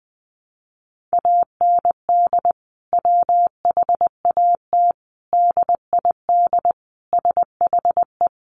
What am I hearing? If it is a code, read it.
Morse code, heard as a keyed tone, about 20 wpm, 700 Hz. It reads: AND WHAT DID SHE
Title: Sentences from Top 200 Words 20wpm